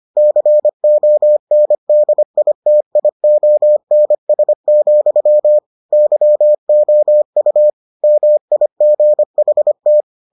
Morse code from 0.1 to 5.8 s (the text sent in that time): CONDITIONS,